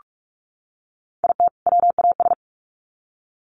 Morse code, heard as a keyed tone, about 44 wpm, 700 Hz. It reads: IT WAS
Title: Sets of 2 Words 45wpm